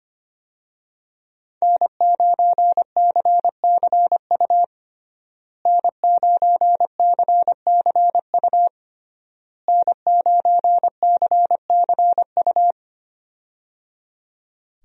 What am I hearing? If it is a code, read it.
Morse code, heard as a keyed tone, about 25 wpm, 700 Hz. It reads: N9CCU N9CCU N9CCU
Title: USA Call Signs - 5 Characters - Repeated 3x - 25wpm